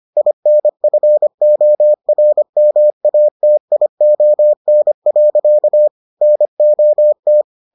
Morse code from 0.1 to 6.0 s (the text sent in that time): INFORMATION.